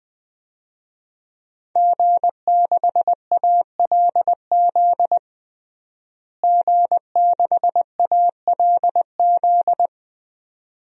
Morse code, heard as a keyed tone, about 20 wpm, 700 Hz. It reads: G6ALZ G6ALZ
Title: DXCC-100 Call Signs - 5 Characters - Repeated 2x - 20wpm